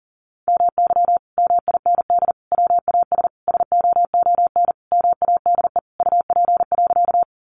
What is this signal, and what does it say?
Morse code, heard as a keyed tone, about 40 wpm, 700 Hz. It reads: MY MIND WAS SOON MADE UP.
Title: Sentences from Top 500 Words 40wpm